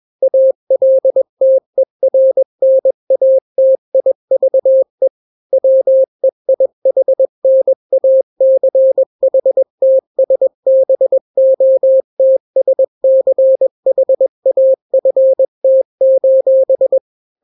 Morse code, heard as a keyed tone, about 21 wpm, 530 Hz. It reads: ALTERNATIVE WEIHNACHTSBOTSCHAFT:
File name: daily / 20wpm